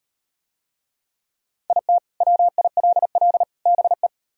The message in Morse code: IT WILL BE

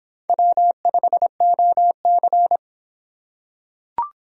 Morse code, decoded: W5OC

